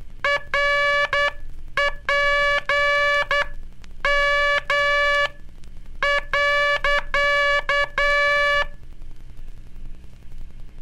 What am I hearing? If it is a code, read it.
Morse code, heard as a keyed tone, about 10 wpm, spaced as for 8 wpm, 1700 Hz, with noise. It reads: RPM.
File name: cursos / Curso CW mp3 / CW Aula - 15